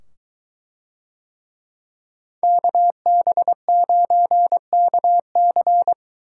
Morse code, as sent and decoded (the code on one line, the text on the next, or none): -.- -... ----. -.- -.-.
KB9KC